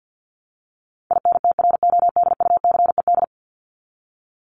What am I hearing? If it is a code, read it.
Morse code, heard as a keyed tone, about 50 wpm, 700 Hz. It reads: INTRODUCED
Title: ED Suffix - Top 100 Words with Base Words 50wpm